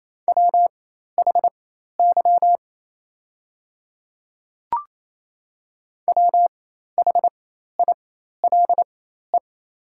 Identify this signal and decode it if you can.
Morse code, heard as a keyed tone, about 28 wpm, spaced as for 13 wpm, 700 Hz. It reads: WHY WHILE
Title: 4x Character Spacing - Top 300 Words - 28wpm